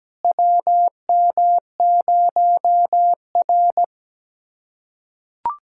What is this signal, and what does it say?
Morse code, heard as a keyed tone, about 17 wpm, 700 Hz. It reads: WM0R
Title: USA Callsign DE Callsign - 4 Characters - 17wpm